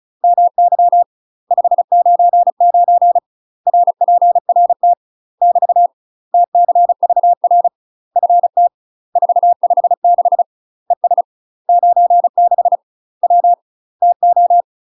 Morse code, heard as a keyed tone, about 36 wpm, 700 Hz. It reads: MY 599 RPRT <BT> TCVR FT 456 ES 96 W TO
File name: UK MASTER QSOs TRAINING / 35WPM / UK MASTER QSOs 0018